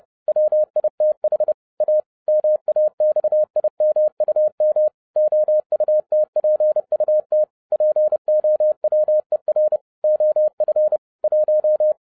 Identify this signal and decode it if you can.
Morse code, heard as a keyed tone, about 30 wpm, 610 Hz. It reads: WITH A MAXIMUM OUTPUT POWER OF 1
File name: 2011-07-mp3 / n40 EXCITING QRP CW OPERATION 30wpm